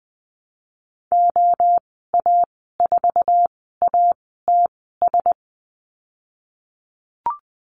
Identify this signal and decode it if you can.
Morse code, heard as a keyed tone, about 20 wpm, spaced as for 15 wpm, 700 Hz. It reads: OA4ATS
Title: 076 Lesson 13g - Letter L - Call Signs 20wpm